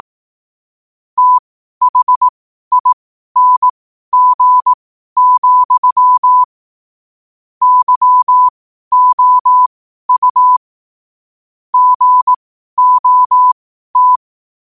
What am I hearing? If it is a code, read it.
Morse code, heard as a keyed tone, about 18 wpm, spaced as for 12 wpm, 1000 Hz. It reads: THING, YOU GOT